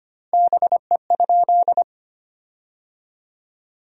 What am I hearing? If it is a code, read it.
Morse code, heard as a keyed tone, about 25 wpm, 700 Hz. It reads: BE?
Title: 2x Word-Spacing - Sentences from Top 100 Words - 25wpm